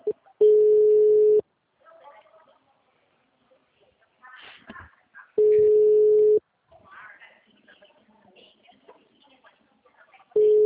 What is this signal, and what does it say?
Ringback tone (Europe, 425 Hz), with speech.
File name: out- -2005- - - .122